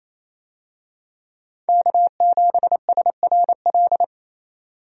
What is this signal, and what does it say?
Morse code, heard as a keyed tone, about 28 wpm, 700 Hz. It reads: K7SRL
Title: USA Call Signs - 5 Characters - 28wpm